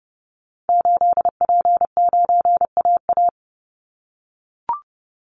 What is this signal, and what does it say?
Morse code, heard as a keyed tone, about 30 wpm, 700 Hz. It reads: 8P9AA
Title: Call Signs - 30wpm